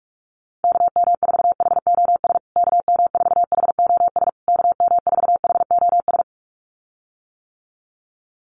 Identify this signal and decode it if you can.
Morse code, heard as a keyed tone, about 44 wpm, 700 Hz. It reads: KM4HOS KM4HOS KM4HOS
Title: USA Call Signs Repeated 3x 45wpm